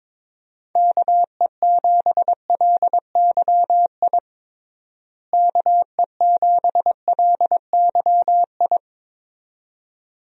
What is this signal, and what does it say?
Morse code, heard as a keyed tone, about 22 wpm, 700 Hz. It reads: KE7LYI KE7LYI
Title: USA Call Signs - 6 Characters - Repeated 2x - 22wpm